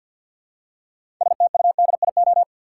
Morse code, heard as a keyed tone, about 51 wpm, 700 Hz. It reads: STUDIO
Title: Six Letter Words - 50wpm